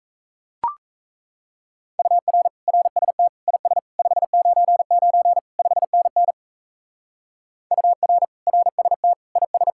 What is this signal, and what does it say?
Morse code, heard as a keyed tone, about 33 wpm, 700 Hz. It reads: UR RST IS 599 5NN UR RST IS
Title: Speed-Racing Common QSO Phrases - 28wpm